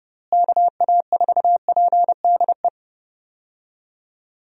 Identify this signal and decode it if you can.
Morse code, heard as a keyed tone, about 30 wpm, 700 Hz. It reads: KA4PDE
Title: USA Call Signs - 6 Characters - Repeated 2x - 30wpm